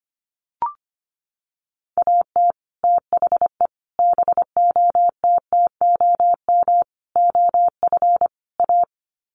Morse code, heard as a keyed tone, about 25 wpm, 700 Hz. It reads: AT THE BOTTOM OF A
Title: Sets of 5 Words Encore2 25wpm